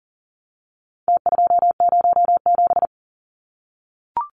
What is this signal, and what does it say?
Morse code, heard as a keyed tone, about 40 wpm, 700 Hz. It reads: T207
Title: Numbers - Contest Style - 4-Digit 40wpm